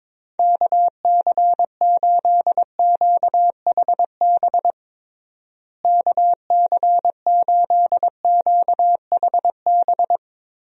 Morse code, heard as a keyed tone, about 21 wpm, 700 Hz. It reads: KC8QHB KC8QHB
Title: USA Call Signs - 6 Characters - Repeated 3x - 22wpm